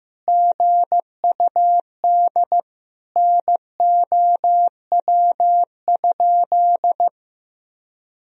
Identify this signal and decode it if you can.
Morse code, heard as a keyed tone, about 15 wpm, 700 Hz. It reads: GUD NOW?
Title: Sentences from Top 200 Words and Abbreviations 15wpm